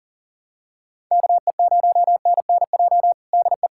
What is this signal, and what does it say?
Morse code, heard as a keyed tone, about 40 wpm, 700 Hz. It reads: KE0NNJ DE
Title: USA Callsign DE Callsign - 6 Characters - 40wpm